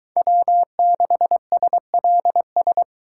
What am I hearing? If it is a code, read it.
Morse code, heard as a keyed tone, about 24 wpm, 700 Hz. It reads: W6SLS